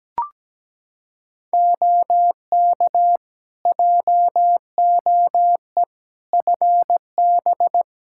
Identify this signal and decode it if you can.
Morse code, heard as a keyed tone, about 17 wpm, 700 Hz. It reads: OK JOE FB
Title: K1USN SST QSO Practice 17wpm